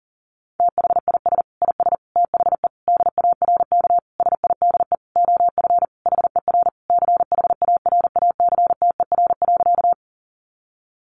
Morse code, heard as a keyed tone, about 40 wpm, 700 Hz. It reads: THIS IS THE DARK SIDE OF HER CHARACTER.
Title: Sentences from Top 1200 Words 40wpm